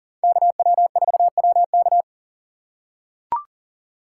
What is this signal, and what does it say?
Morse code, heard as a keyed tone, about 40 wpm, 700 Hz. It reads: KW4WK
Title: USA Call Signs - 4x Character Spacing - 40wpm